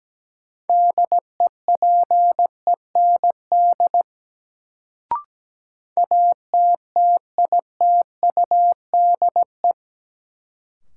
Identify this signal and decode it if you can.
Morse code, heard as a keyed tone, about 17 wpm, 700 Hz. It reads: DEPEND ATTITUDE
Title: Sentences from Top 1300 Words - New Word Drill Alt - 17wpm